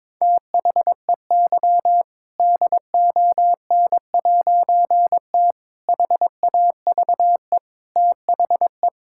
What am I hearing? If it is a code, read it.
Morse code, heard as a keyed tone, about 22 wpm, 700 Hz. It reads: THEY DON'T HAVE THE
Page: morse-code-ninja-practice-22wpm